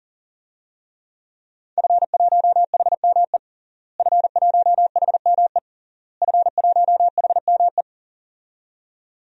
Morse code, heard as a keyed tone, about 40 wpm, 700 Hz. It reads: F1HME F1HME F1HME
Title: DXCC-100 Call Signs - 5 Characters - Repeated 3x - 40wpm